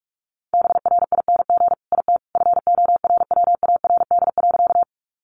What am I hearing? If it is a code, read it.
Morse code, heard as a keyed tone, about 44 wpm, 700 Hz. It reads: BRING IT FORWARD.
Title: Sentences from Top 1300 Words 45wpm